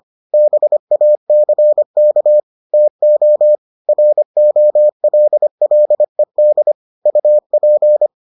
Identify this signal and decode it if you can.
Morse code, heard as a keyed tone, about 25 wpm, 600 Hz. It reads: BACK TO ROLLED UP